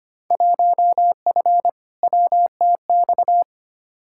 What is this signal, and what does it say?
Morse code, heard as a keyed tone, about 25 wpm, 700 Hz. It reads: 1F WTX
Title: Field Day Exchanges - 25wpm